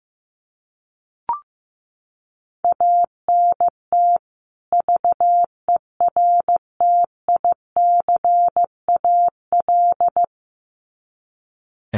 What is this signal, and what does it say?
Morse code, heard as a keyed tone, about 15 wpm, 700 Hz. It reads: ANT VERTICAL
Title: Common QSO Phrases 15wpm